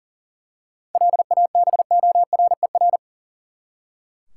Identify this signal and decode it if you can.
Morse code, heard as a keyed tone, about 37 wpm, 700 Hz. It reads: LABORER
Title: Practice QSO - Exchange 3 - More Info - 40wpm